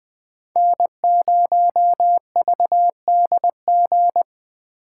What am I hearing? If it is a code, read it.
Morse code, heard as a keyed tone, about 20 wpm, 700 Hz. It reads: N0VDG